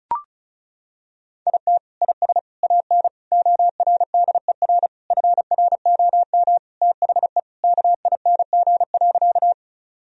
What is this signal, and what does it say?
Morse code, heard as a keyed tone, about 34 wpm, 700 Hz. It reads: IT IS AN ORDER FROM THE KING.
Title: Sentences from Top 1000 Words - Review Part 2 - 35wpm